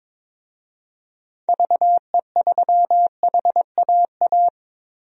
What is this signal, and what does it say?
Morse code, heard as a keyed tone, about 22 wpm, 700 Hz. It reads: VE3HAA
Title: DXCC-100 Call Signs - 6 Characters - Repeated 2x - 22wpm